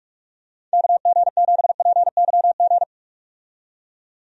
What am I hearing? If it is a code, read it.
Morse code, heard as a keyed tone, about 43 wpm, 700 Hz. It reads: KG7PYG
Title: USA Call Signs - 6 Characters - 45wpm